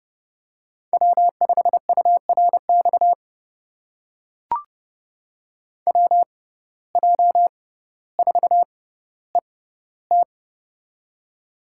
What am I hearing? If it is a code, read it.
Morse code, heard as a keyed tone, about 30 wpm, spaced as for 15 wpm, 700 Hz. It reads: W5URX WJ4ET
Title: USA Call Signs - 5 Characters - 6x Character Spacing - 30wpm